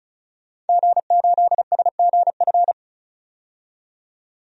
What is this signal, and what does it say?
Morse code, heard as a keyed tone, about 36 wpm, 700 Hz. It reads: G8SGF
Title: DXCC-100 Call Signs - 5 Characters - 35wpm